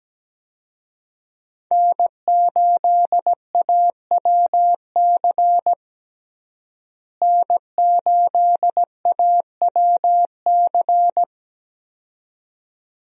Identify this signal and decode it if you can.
Morse code, heard as a keyed tone, about 17 wpm, 700 Hz. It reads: N8AWC N8AWC
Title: USA Call Signs - 5 Characters - Repeated 2x - 17wpm